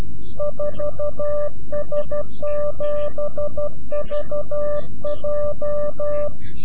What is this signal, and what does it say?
Morse code, heard as a keyed tone, about 12 wpm, 590 Hz. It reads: 4S7VJ
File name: CW-QSO / CQ Call / CQ-QSO